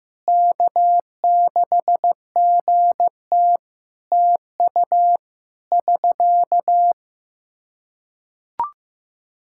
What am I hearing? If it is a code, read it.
Morse code, heard as a keyed tone, about 15 wpm, 700 Hz. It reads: K6GT TU <SK>
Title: Practice QSO - Exchange 4 - Ending - 15wpm